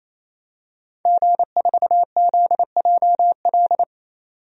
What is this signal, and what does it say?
Morse code, heard as a keyed tone, about 28 wpm, 700 Hz. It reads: G4ZJL
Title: DXCC-100 Call Signs - 5 Characters - 28wpm